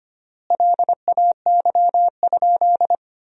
Morse code, heard as a keyed tone, about 25 wpm, 700 Hz. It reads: LAY?
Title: 3x Word-Spacing - Sentences from Top 400 Words - 25wpm